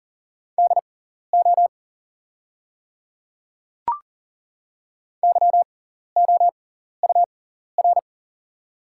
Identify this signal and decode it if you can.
Morse code, heard as a keyed tone, about 40 wpm, spaced as for 14 wpm, 690 Hz. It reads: DO YOUR
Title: 6x Character Spacing - Top 100 Words - 40wpm